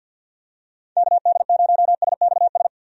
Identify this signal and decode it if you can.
Morse code, heard as a keyed tone, about 52 wpm, 700 Hz. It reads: KD0SXS
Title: USA Call Signs - 6 Characters - 50wpm